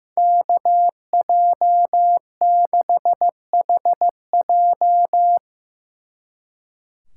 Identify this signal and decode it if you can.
Morse code, heard as a keyed tone, about 17 wpm, 700 Hz. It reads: KJ6HJ
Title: Speed Racing - USA 2x2 Call Signs 15wpm